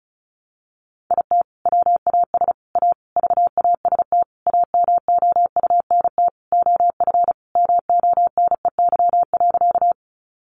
Morse code, heard as a keyed tone, about 34 wpm, 700 Hz. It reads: IT WAS A VAST AMOUNT OF MONEY.
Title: Sentences from Top 1300 Words 35wpm